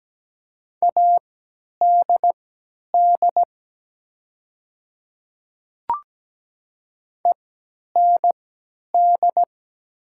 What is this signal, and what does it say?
Morse code, heard as a keyed tone, about 17 wpm, spaced as for 10 wpm, 700 Hz. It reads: ADD END
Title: 3x Character Spacing - Top 200 Words - 17wpm